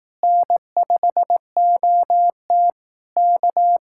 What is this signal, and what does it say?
Morse code, heard as a keyed tone, about 18 wpm, 700 Hz. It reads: N5OT K
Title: Speed-Racing Common QSO Phrases Encore - 15wpm